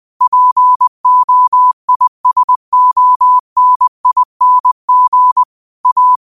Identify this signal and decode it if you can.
Morse code, heard as a keyed tone, about 20 wpm, 1000 Hz. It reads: POISONING A